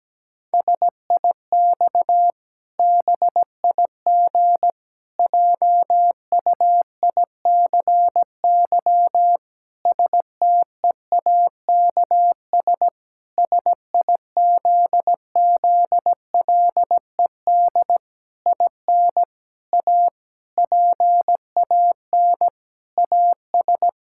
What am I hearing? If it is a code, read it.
Morse code, heard as a keyed tone, about 17 wpm, 700 Hz. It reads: SIX BIG JUICY STEAKS SIZZLED IN A PAN AS
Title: Pangram Challenge v2 - 17wpm